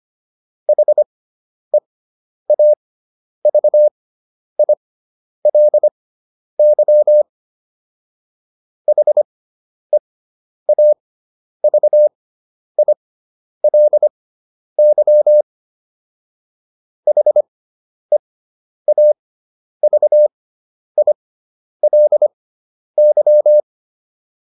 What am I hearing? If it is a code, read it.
Morse code, heard as a keyed tone, about 25 wpm, spaced as for 10 wpm, 600 Hz. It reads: HEAVILY HEAVILY HEAVILY
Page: fundamental-practice-files